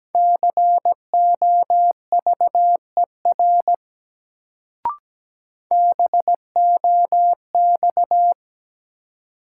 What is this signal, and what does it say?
Morse code, heard as a keyed tone, about 17 wpm, 700 Hz. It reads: COVER BOX
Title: Sentences from Top 500 Words - New Word Drill - 17wpm